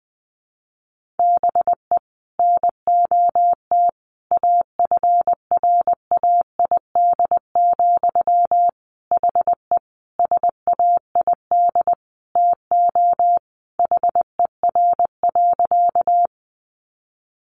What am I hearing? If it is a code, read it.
Morse code, heard as a keyed tone, about 20 wpm, 700 Hz. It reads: BE NOT AFRAID, HE SAID TO HER.
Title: Sentences from Top 1500 Words 20wpm